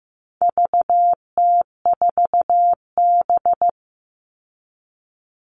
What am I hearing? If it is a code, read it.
Morse code, heard as a keyed tone, about 15 wpm, 700 Hz. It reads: VT4B
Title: All Cut Numbers - 4-Digit 15wpm